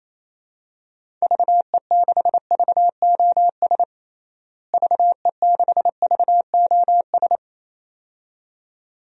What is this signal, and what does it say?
Morse code, heard as a keyed tone, about 28 wpm, 700 Hz. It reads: VE6VOS VE6VOS
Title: DXCC-100 Call Signs - 6 Characters - Repeated 2x - 28wpm